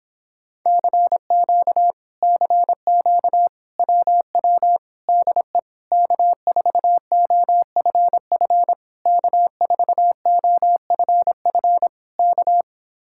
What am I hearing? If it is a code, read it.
Morse code, heard as a keyed tone, about 26 wpm, 700 Hz. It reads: CQ CQ WW DE K4OFF K4OFF K